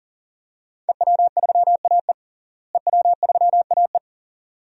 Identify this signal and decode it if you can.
Morse code, heard as a keyed tone, about 40 wpm, 700 Hz. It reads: EW3AE EW3AE
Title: DXCC-100 Call Signs - 5 Characters - Repeated 2x - 40wpm